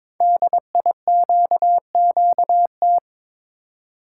Morse code, heard as a keyed tone, about 22 wpm, 700 Hz. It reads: DIQQT
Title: Code Groups - 5 Letters - 22wpm